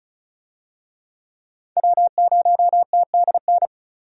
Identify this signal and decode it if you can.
Morse code, heard as a keyed tone, about 36 wpm, 700 Hz. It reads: W0TDN
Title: USA Call Signs - 5 Characters - 35wpm